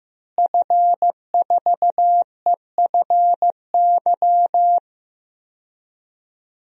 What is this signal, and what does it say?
Morse code, heard as a keyed tone, about 15 wpm, 700 Hz. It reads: F4EFY
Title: DXCC-100 Call Signs - 5 Characters - Repeated 3x - 15wpm